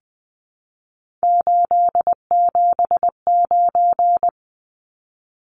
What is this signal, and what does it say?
Morse code, heard as a keyed tone, about 20 wpm, 700 Hz. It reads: 879
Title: Numbers - 3-Digit 20wpm